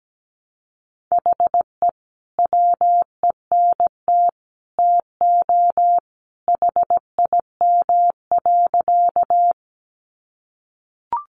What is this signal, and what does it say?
Morse code, heard as a keyed tone, about 17 wpm, 700 Hz. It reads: HE WENT TO HIM.
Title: Sentences from Top 200 Words 17wpm